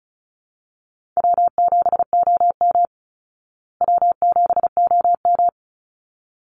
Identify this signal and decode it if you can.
Morse code, heard as a keyed tone, about 34 wpm, 700 Hz. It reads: W7OM W7OM
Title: Easy Call Signs Repeated 2x - 35wpm